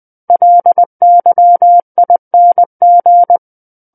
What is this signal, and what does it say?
Morse code, heard as a keyed tone, about 20 wpm, 700 Hz. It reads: LYING
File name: 20 WPM